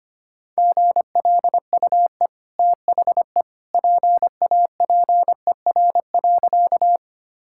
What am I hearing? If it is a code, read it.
Morse code, heard as a keyed tone, about 25 wpm, 700 Hz. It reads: GLUE THE PAPER.
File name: Easy Sentences with Hints - 25wpm